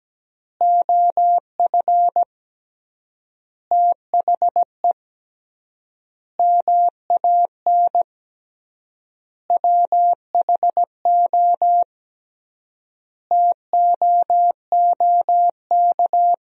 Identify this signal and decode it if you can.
Morse code, heard as a keyed tone, about 17 wpm, 700 Hz. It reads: OF THE MAN WHO TOOK
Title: 3x Word-Spacing - Sentences from Top 300 Words - 17wpm